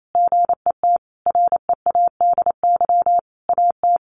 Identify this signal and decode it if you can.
Morse code, heard as a keyed tone, about 28 wpm, 700 Hz. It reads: GET READY AT
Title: Sentences from Top 1100 Words 28wpm